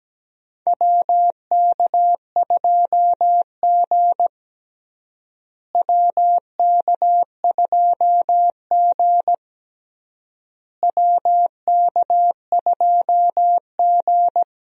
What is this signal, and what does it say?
Morse code, heard as a keyed tone, about 17 wpm, 700 Hz. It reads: WK2G WK2G WK2G
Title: USA Call Signs - 4 Characters - Repeated 3x - 17wpm